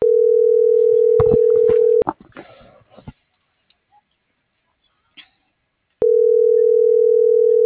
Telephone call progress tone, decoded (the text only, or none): ringback tone